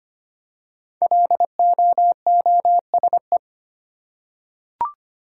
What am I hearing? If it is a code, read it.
Morse code, heard as a keyed tone, about 25 wpm, 700 Hz. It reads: LOOSE
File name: 5 Letter Words - First 3 Letters Given - 25wpm